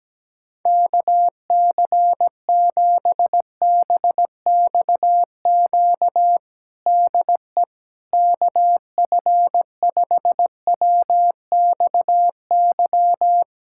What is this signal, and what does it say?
Morse code, heard as a keyed tone, about 17 wpm, 700 Hz. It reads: KC7BXQ DE KF5WXY
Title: USA Callsign DE Callsign 17wpm